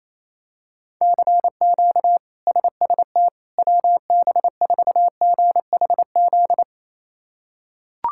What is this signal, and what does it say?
Morse code, heard as a keyed tone, about 27 wpm, 700 Hz. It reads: CQ SST WB4GHZ
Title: K1USN SST QSO Practice - 28wpm